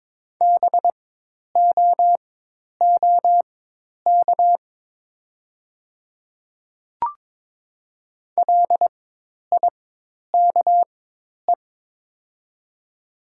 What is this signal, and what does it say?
Morse code, heard as a keyed tone, about 23 wpm, spaced as for 10 wpm, 700 Hz. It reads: BOOK LIKE
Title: 4x Character Spacing - Top 300 Words - 22wpm